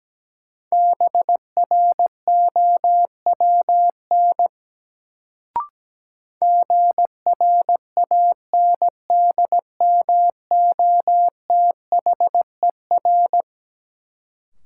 Morse code, heard as a keyed tone, about 17 wpm, 700 Hz. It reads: BROWN GRANDMOTHER
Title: Sentences from Top 1200 Words - New Word Drill Alt - 17wpm